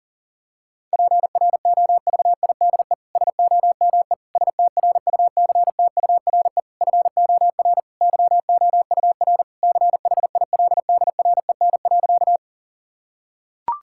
Morse code, heard as a keyed tone, about 40 wpm, 700 Hz. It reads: PROVIDE SOME STRUCTURE FOR YOUR CHILDREN.
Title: Sentences from Top 1000 Words - Review Part 6 - 40wpm